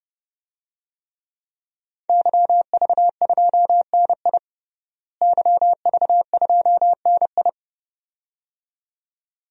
Morse code, heard as a keyed tone, about 30 wpm, 700 Hz. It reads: YV2NI YV2NI
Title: DXCC-100 Call Signs - 5 Characters - Repeated 2x - 30wpm